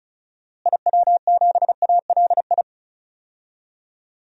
Morse code, heard as a keyed tone, about 34 wpm, 700 Hz. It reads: IW7ALI
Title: DXCC-100 Call Signs - 6 Characters - Repeated 3x - 35wpm